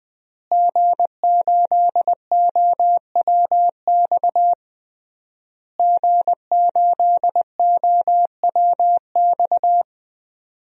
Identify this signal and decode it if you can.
Morse code, heard as a keyed tone, about 20 wpm, 700 Hz. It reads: G8OWX G8OWX
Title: DXCC-100 Call Signs - 5 Characters - Repeated 3x - 20wpm